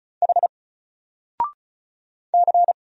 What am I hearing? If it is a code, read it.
Morse code, heard as a keyed tone, about 34 wpm, 700 Hz. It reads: H C
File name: Single-Letter-Number-35wpm